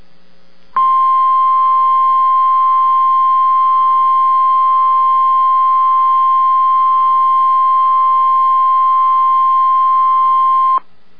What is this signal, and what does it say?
Emergency Alert System (SAME) tones (the 1050 Hz attention tone). It is heard from 0.7 to 10.8 s.